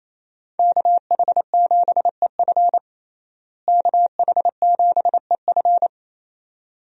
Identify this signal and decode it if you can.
Morse code, heard as a keyed tone, about 28 wpm, 700 Hz. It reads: KH7EF KH7EF
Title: USA Call Signs - 5 Characters - Repeated 2x - 28wpm